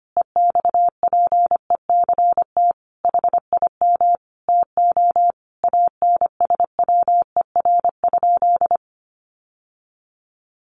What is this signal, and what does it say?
Morse code, heard as a keyed tone, about 25 wpm, 700 Hz. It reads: EXPECT HIM TO ANSWER?
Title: Sentences from Top 1100 Words 25wpm